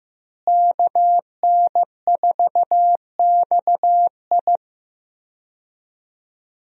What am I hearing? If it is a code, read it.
Morse code, heard as a keyed tone, about 15 wpm, 700 Hz. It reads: KN4XI
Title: USA Call Signs - 5 Characters - Repeated 3x - 15wpm